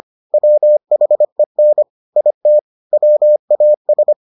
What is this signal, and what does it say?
Morse code, heard as a keyed tone, about 25 wpm, 600 Hz. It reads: WHEN IT WAS